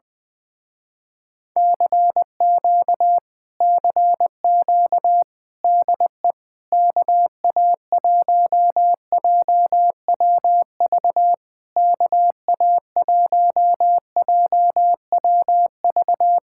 Morse code, heard as a keyed tone, about 20 wpm, 700 Hz. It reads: CQ CQ DE KA1JWV KA1JWV
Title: Speed-Racing Common QSO Phrases - 17wpm